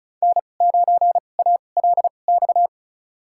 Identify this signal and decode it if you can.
Morse code, heard as a keyed tone, about 36 wpm, spaced as for 25 wpm, 700 Hz. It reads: N9ALX
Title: USA Call Signs - 5 Characters - 2x Character Spacing - 35wpm